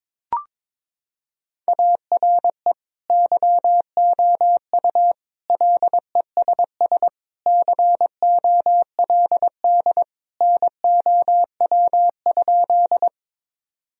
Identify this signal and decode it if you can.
Morse code, heard as a keyed tone, about 22 wpm, 700 Hz. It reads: ARE YOU LESS COLD NOW?
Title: Sentences from Top 1000 Words - Review Part 2 - 22wpm